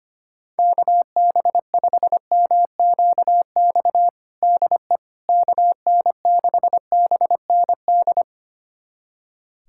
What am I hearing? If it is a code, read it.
Morse code, heard as a keyed tone, about 25 wpm, 700 Hz. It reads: KB5MQX DE KN6BND
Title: USA Callsign DE Callsign 25wpm